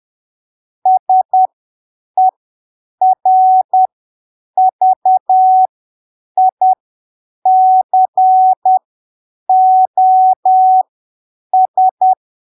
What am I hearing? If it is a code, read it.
Morse code, heard as a keyed tone, about 10 wpm, spaced as for 7 wpm, 750 Hz. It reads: SERVICOS
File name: arquivos / texto / texto 12 10ppm